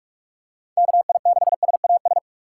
Morse code, heard as a keyed tone, about 47 wpm, 700 Hz. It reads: KI6SAS
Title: USA Call Signs - 6 Characters - 4x Character Spacing - 45wpm